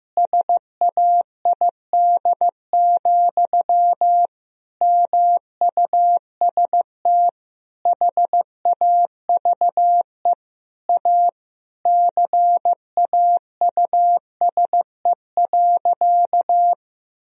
Morse code, heard as a keyed tone, about 15 wpm, 700 Hz. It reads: SAID, MUST HAVE A CAUSE.
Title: Sentences from Top 1000 Words - Review Part 1 - 15wpm